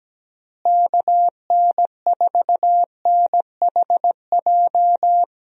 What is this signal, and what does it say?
Morse code, heard as a keyed tone, about 17 wpm, 700 Hz. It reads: KN4NHJ
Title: USA Call Signs - 6 Characters - Repeated 2x - 17wpm